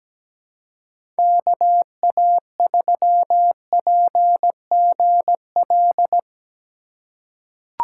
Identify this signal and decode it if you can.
Morse code, heard as a keyed tone, about 17 wpm, 700 Hz. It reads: KA3PGL
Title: USA Call Signs - 6 Characters - 4x Character Spacing - 17wpm